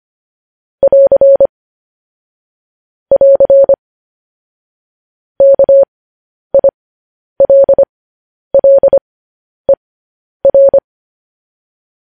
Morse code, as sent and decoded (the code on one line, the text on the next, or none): .-.-. / .-.-. / -.- .. .-.. .-.. .-.
<AR> <AR> KILLER